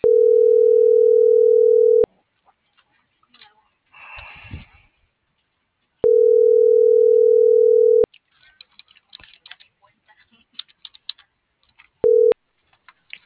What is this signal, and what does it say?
Ringback tone (North America), 440 + 480 Hz.